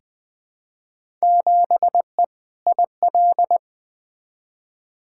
Morse code, heard as a keyed tone, about 20 wpm, 700 Hz. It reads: 7E IL
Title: Field Day Exchanges Encore - 20wpm